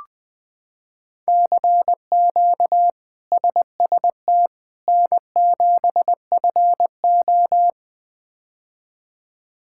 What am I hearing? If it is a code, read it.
Morse code, heard as a keyed tone, about 20 wpm, 700 Hz. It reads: CQ SST N7FO
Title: K1USN SST QSO Practice - 20wpm